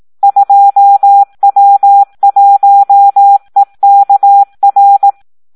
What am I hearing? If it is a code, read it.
Morse code, heard as a keyed tone, about 18 wpm, 800 Hz. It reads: 2W1EKR
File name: Audio practica CW / SD 1 18